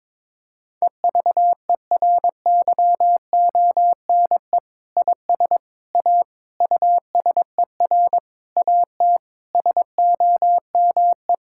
Morse code, heard as a keyed tone, about 22 wpm, 700 Hz. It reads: EVERYONE IS A USER AT SOME
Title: Sentences from Top 1000 Words - Review Part 5 - 22wpm